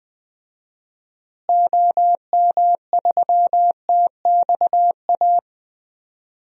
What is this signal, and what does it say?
Morse code, heard as a keyed tone, about 20 wpm, 700 Hz. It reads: OM3TXA
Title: DXCC-100 Call Signs - 6 Characters - Repeated 2x - 20wpm